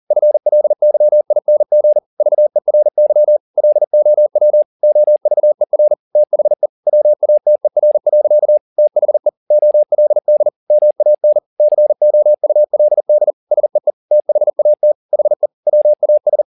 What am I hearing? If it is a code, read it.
Morse code, heard as a keyed tone, about 40 wpm, 600 Hz. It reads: FLYING VERY LOW OVER THE WATER. THE OLD MAN COULD SEE THAT HE WAS